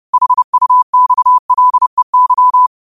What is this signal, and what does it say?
Morse code, heard as a keyed tone, about 30 wpm, 1000 Hz. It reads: HUXLEY
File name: qotdfiles / 30wpm